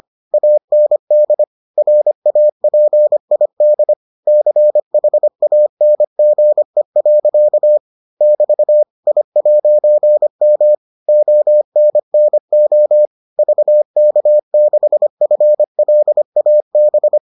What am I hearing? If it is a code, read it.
Morse code, heard as a keyed tone, about 25 wpm, 600 Hz. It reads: AND RAPID CHANGE. <BT> I'M ONNO VK6FLAB